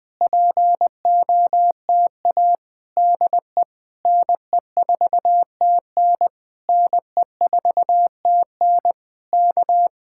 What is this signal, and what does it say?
Morse code, heard as a keyed tone, about 20 wpm, 700 Hz. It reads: POTA DE NE4TN NE4TN K
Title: Speed-Racing Common QSO Phrases Encore - 20wpm